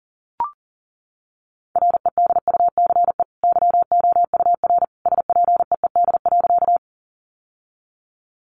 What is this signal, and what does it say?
Morse code, heard as a keyed tone, about 40 wpm, 700 Hz. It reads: REDUCE YOUR SPEED.
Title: Sentences from Top 1100 Words 40wpm